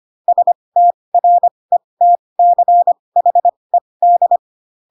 Morse code, heard as a keyed tone, about 25 wpm, spaced as for 20 wpm, 700 Hz. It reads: STRETCHED